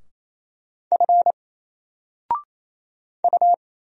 Morse code, heard as a keyed tone, about 28 wpm, 700 Hz. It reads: F U
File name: Single-Letter-Number-28wpm